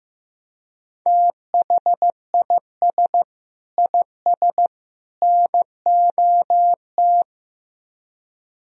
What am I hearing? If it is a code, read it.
Morse code, heard as a keyed tone, about 15 wpm, 700 Hz. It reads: THIS IS NOT
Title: Speed-Racing - Sets of 3 Words 15wpm